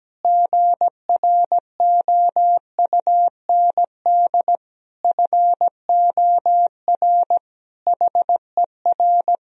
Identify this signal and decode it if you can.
Morse code, heard as a keyed tone, about 17 wpm, 700 Hz. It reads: GROUND FOR HER
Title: Sentences from Top 1000 Words - Review Part 2 - 17wpm